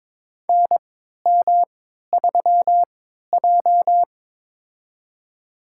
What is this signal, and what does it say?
Morse code, heard as a keyed tone, about 22 wpm, spaced as for 12 wpm, 700 Hz. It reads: NM3J
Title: USA Call Signs - 4 Characters - 3x Character Spacing - 22wpm